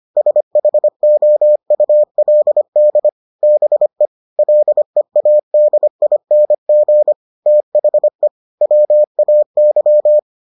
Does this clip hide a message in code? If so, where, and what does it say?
Yes: Morse code from 0.2 to 10.2 s, heard as a keyed tone, about 25 wpm, 600 Hz. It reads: SHOULD BE LEADING THE WAY